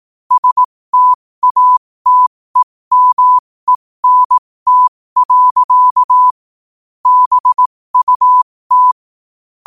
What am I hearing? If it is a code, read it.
Morse code, heard as a keyed tone, about 18 wpm, spaced as for 15 wpm, 1000 Hz. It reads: STATEMENT. BUT
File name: qotdfiles / 15wpm